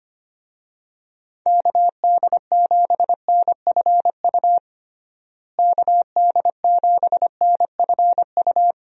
Morse code, heard as a keyed tone, about 25 wpm, 700 Hz. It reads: KD7NFU KD7NFU
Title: USA Call Signs - 6 Characters - Repeated 2x - 25wpm